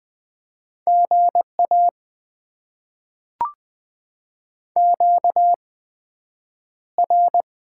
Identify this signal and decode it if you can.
Morse code, heard as a keyed tone, about 20 wpm, spaced as for 9 wpm, 700 Hz. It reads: GA QR
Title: Top 20 QSO Elements - 8x Character Spacing - 20wpm